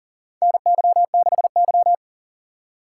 Morse code, heard as a keyed tone, about 40 wpm, 700 Hz. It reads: NY6Y
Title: K1USN SST QSO Practice - 40wpm